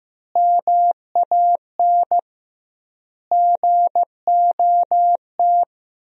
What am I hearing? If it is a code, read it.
Morse code, heard as a keyed tone, about 16 wpm, 700 Hz. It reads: MAN GOT
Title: 2x Word-Spacing - Sentences from Top 300 Words - 15wpm